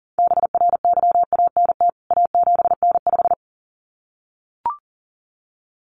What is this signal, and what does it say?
Morse code, heard as a keyed tone, about 40 wpm, 700 Hz. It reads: BRYANT A7N5
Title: CWT Exchange 40wpm